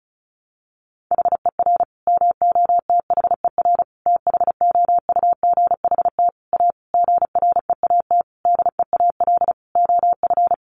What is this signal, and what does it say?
Morse code, heard as a keyed tone, about 34 wpm, 700 Hz. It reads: HER MOTHER THOUGHT A GREAT DEAL OF
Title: Sentences from Top 800 Words 35wpm